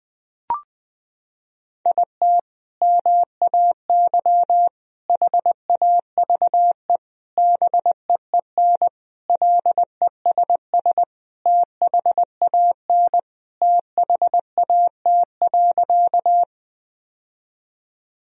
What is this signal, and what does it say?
Morse code, heard as a keyed tone, about 20 wpm, 700 Hz. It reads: IT MAY HAVE BEEN LESS THAN THAT.
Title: Sentences from Top 1000 Words - Review Part 2 - 20wpm